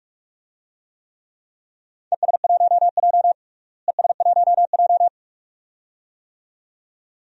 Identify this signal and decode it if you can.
Morse code, heard as a keyed tone, about 41 wpm, 700 Hz. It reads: ES1J ES1J
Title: DXCC-100 Call Signs - 4 Characters - Repeated 2x - 45wpm